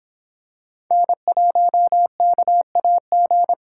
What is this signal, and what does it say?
Morse code, heard as a keyed tone, about 26 wpm, 700 Hz. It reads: N1KAG